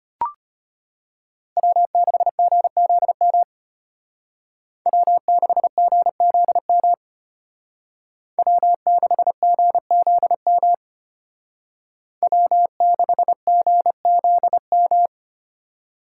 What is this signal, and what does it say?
Morse code, heard as a keyed tone, about 30 wpm, 700 Hz. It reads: W6GZM W6GZM W6GZM W6GZM
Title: Speed Racing - USA 1x3 Call Signs 25wpm